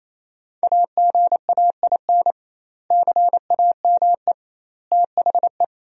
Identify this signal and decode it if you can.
Morse code, heard as a keyed tone, about 28 wpm, 700 Hz. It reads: AGAIN CAME THE
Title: 2x Word-Spacing - Sentences from Top 500 Words - 28wpm